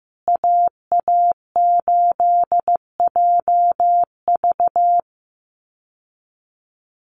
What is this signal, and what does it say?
Morse code, heard as a keyed tone, about 15 wpm, 700 Hz. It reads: AA8JV
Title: USA Call Signs Repeated 2x 15wpm